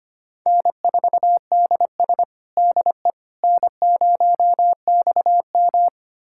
Morse code, heard as a keyed tone, about 25 wpm, 700 Hz. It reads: N4DS DE N0XM
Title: USA Callsign DE Callsign - 4 Characters - 25wpm